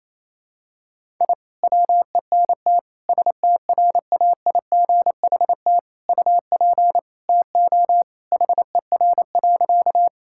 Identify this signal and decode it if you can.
Morse code, heard as a keyed tone, about 28 wpm, 700 Hz. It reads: I WENT STRAIGHT UP TO HER.
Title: Sentences from Top 1000 Words - Review Part 6 - 28wpm